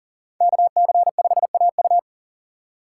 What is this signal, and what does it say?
Morse code, heard as a keyed tone, about 40 wpm, 700 Hz. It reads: KC5AU
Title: USA Call Signs - 5 Characters - 4x Character Spacing - 40wpm